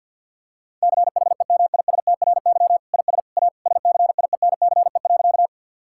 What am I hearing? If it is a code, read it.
Morse code, heard as a keyed tone, about 50 wpm, 700 Hz. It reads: CHEMISTRY IS A SCIENCE.
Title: Sentences from Top 1000 Words - Review Part 6 - 50wpm